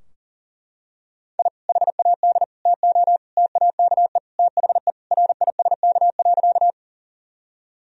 Morse code, heard as a keyed tone, about 37 wpm, 700 Hz. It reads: I HAD TO TAKE THE RISK.
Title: Sentences from Top 1000 Words - Review Part 4 - 40wpm